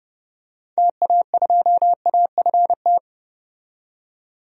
Morse code, heard as a keyed tone, about 30 wpm, 700 Hz. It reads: TA2AFT
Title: DXCC-100 Call Signs - 6 Characters - 30wpm